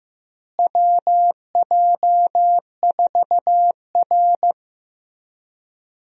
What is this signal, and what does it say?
Morse code, heard as a keyed tone, about 15 wpm, 700 Hz. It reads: WJ4R
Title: USA Call Signs - 4 Characters - Repeated 3x - 15wpm